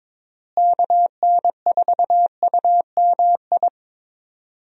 Morse code, heard as a keyed tone, about 22 wpm, 700 Hz. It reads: KN4UMI
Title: USA Call Signs - 6 Characters - Repeated 3x - 22wpm